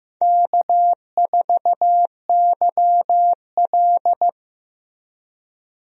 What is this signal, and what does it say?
Morse code, heard as a keyed tone, about 15 wpm, 700 Hz. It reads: K4YL
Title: USA Call Signs - 4 Characters - Repeated 2x - 15wpm